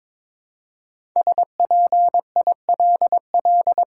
Morse code, heard as a keyed tone, about 22 wpm, 700 Hz. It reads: SPILL
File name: 5 Letter Words - First 3 Letters Given - 22wpm